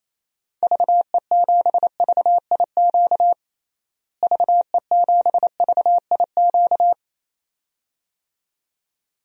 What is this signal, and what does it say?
Morse code, heard as a keyed tone, about 28 wpm, 700 Hz. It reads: VE7VIQ VE7VIQ
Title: DXCC-100 Call Signs - 6 Characters - Repeated 2x - 28wpm